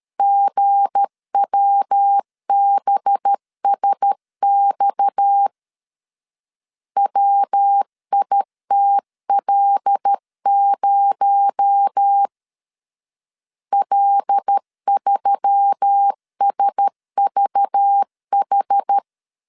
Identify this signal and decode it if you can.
Morse code, heard as a keyed tone, about 13 wpm, 800 Hz. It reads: GWBSX WITL0 L3SVH
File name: LEZIONI / 7 GF9Z / 7 G 12wpm 2055